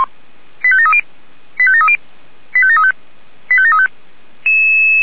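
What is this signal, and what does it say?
Five-tone selective call (ZVEI), heard as quick runs of five tones, tones of about 70 ms. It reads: 85730 85730 85735 85735
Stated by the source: ZVEI five-tone selective call 85722 received by SDR.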